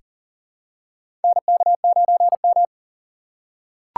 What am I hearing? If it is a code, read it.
Morse code, heard as a keyed tone, about 40 wpm, 700 Hz. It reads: NK9M
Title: USA Call Signs - 4 Characters - 6x Character Spacing - 40wpm